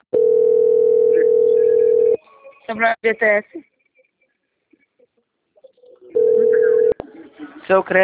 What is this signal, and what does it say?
Ringback tone (North America), 440 + 480 Hz, with speech.
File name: out- - - - - .18596